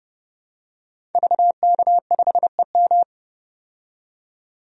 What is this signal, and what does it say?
Morse code, heard as a keyed tone, about 30 wpm, 700 Hz. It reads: VK5EM
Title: DXCC-100 Call Signs - 5 Characters - 30wpm